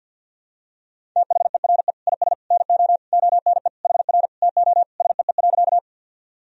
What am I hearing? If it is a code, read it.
Morse code, heard as a keyed tone, about 51 wpm, 700 Hz. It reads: THERE IS NO ONE HR TO SEE.
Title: Sentences from Top 200 Words and Abbreviations 50wpm